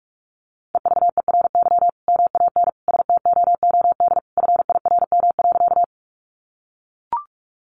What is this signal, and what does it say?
Morse code, heard as a keyed tone, about 44 wpm, 700 Hz. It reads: EVERY MAN STOOD FIRM.
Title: Sentences from Top 1400 Words 45wpm